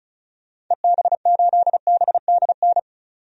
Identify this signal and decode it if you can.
Morse code, heard as a keyed tone, about 34 wpm, 700 Hz. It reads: EB8BDN
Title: DXCC-100 Call Signs - 6 Characters - 35wpm